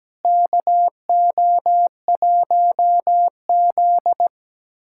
Morse code, heard as a keyed tone, about 17 wpm, 700 Hz. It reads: KO1Z
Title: USA Callsign DE Callsign - 4 Characters - 17wpm